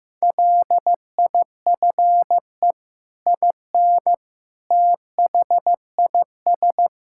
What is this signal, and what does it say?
Morse code, heard as a keyed tone, about 16 wpm, 700 Hz. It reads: LIFE IN THIS
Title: Sentences from Top 1000 Words - Review Part 5 - 15wpm